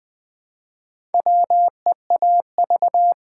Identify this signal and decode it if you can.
Morse code, heard as a keyed tone, about 20 wpm, 700 Hz. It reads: WEAV
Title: Build A Word - 1 Letter At A Time - 5-Letter Words - 20wpm